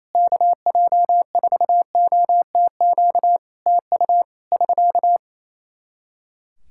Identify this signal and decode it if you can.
Morse code, heard as a keyed tone, about 28 wpm, 700 Hz. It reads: KJ4OTQ TU <SK>
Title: Practice QSO - Exchange 4 - Ending - 28wpm